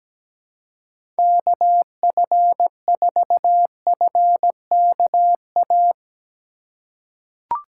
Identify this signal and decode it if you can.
Morse code, heard as a keyed tone, about 17 wpm, 700 Hz. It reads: KF4FKA
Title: USA Call Signs - 6 Characters - 4x Character Spacing - 17wpm